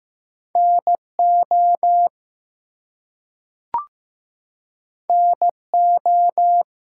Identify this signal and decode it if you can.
Morse code, heard as a keyed tone, about 15 wpm, 700 Hz. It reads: NO NO
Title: Two Letter Words - 15wpm